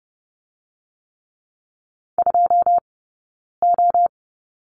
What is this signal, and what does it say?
Morse code, heard as a keyed tone, about 30 wpm, 700 Hz. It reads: 2 O
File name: Lesson 19d - Letter w - Two Characters 30wpm